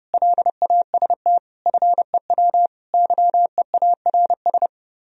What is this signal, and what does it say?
Morse code, heard as a keyed tone, about 37 wpm, 700 Hz. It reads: LAST FEW YEARS
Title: Speed-Racing - Sets of 5 Words 30wpm